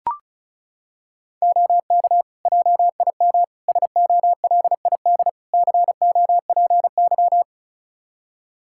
Morse code, heard as a keyed tone, about 36 wpm, 700 Hz. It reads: OK JIM SOLID COPY
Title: Practice QSO - Exchange 3 - More Info - 35wpm